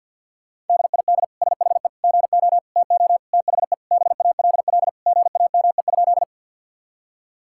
Morse code, heard as a keyed tone, about 52 wpm, 700 Hz. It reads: DID SHE GO TO THE BALL GAME?
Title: Sentences from Top 1000 Words - Review Part 6 - 50wpm